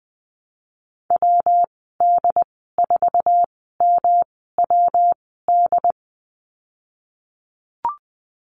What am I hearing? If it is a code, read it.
Morse code, heard as a keyed tone, about 20 wpm, spaced as for 15 wpm, 700 Hz. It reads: WD4MWD
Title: 154 Lesson 25g - Letter p - Call Signs 20wpm